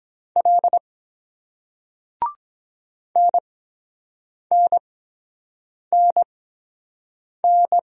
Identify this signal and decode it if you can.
Morse code, heard as a keyed tone, about 23 wpm, 700 Hz. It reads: L N N N N